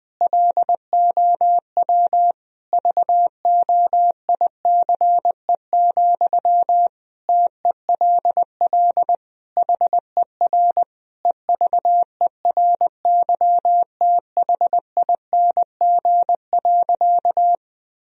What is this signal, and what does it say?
Morse code, heard as a keyed tone, about 21 wpm, 700 Hz. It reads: LOW VOICE, TELL HER EVERYTHING.
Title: Sentences from Top 1000 Words - Review Part 1 - 20wpm